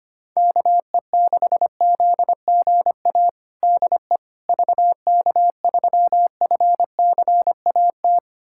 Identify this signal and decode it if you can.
Morse code, heard as a keyed tone, about 25 wpm, 700 Hz. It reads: KE6ZGA DE VK3FCAT
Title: USA Callsign DE Callsign 25wpm